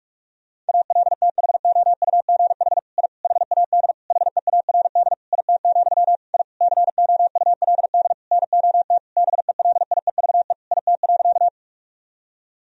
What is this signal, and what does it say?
Morse code, heard as a keyed tone, about 41 wpm, 700 Hz. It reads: ALTHOUGH I HAD HEARD IT, I COULD NOT BELIEVE IT.
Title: Sentences from Top 1000 Words - Review Part 3 - 45wpm